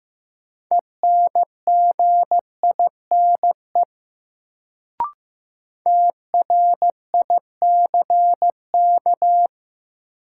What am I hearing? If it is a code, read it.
Morse code, heard as a keyed tone, about 15 wpm, 700 Hz. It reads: ENGINE TRICK
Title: Sentences from Top 1500 Words - New Word Drill Alt - 15wpm